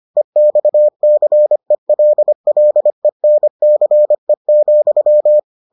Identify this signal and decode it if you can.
Morse code, heard as a keyed tone, about 25 wpm, 600 Hz. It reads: EXCELLENCE,